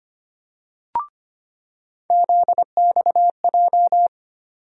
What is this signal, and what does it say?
Morse code, heard as a keyed tone, about 25 wpm, 700 Hz. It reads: ZXJ
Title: USA Call Sign Postfixes - 3 Letters 25wpm